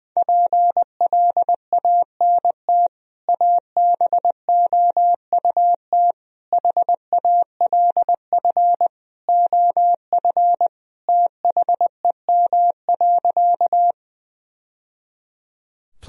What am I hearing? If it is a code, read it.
Morse code, heard as a keyed tone, about 20 wpm, 700 Hz. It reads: PLANT ABOUT HALF OF THEM.
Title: Sentences from Top 1000 Words - Review Part 1 - 20wpm